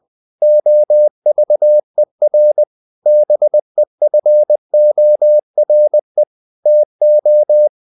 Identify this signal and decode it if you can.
Morse code, heard as a keyed tone, about 20 wpm, 600 Hz. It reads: OVER BEFORE TO
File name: CW / Practice / 500MostCommonEnglishWords20